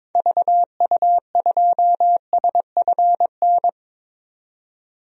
Morse code, heard as a keyed tone, about 22 wpm, 700 Hz. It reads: VU2SFN
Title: DXCC-100 Call Signs - 6 Characters - 22wpm